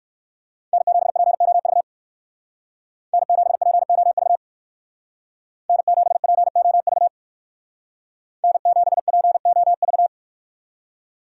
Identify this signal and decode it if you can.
Morse code, heard as a keyed tone, about 64 wpm, 700 Hz. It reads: N7POV N7POV N7POV N7POV